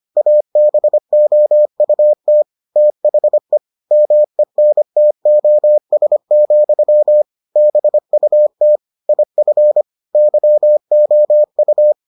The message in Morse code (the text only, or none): ABOUT THE MENTOS, BUT IF YOU